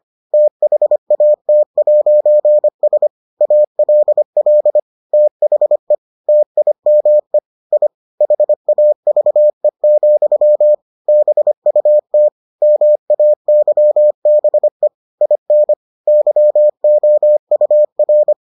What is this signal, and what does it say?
Morse code, heard as a keyed tone, about 25 wpm, 600 Hz. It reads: THAT'S ALL THE TIME I HAVE, BUT MAYBE IN YOUR